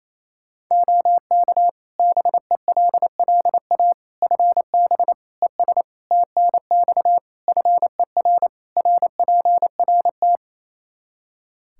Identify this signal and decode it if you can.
Morse code, heard as a keyed tone, about 28 wpm, 700 Hz. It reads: OK BELLA FB ES TNX FER RPRT
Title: K1USN SST QSO Practice 28wpm